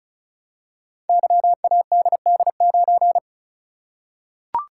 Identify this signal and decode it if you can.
Morse code, heard as a keyed tone, about 36 wpm, 700 Hz. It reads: YADD9
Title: Code Groups - 5 Letter:Numbers - 35wpm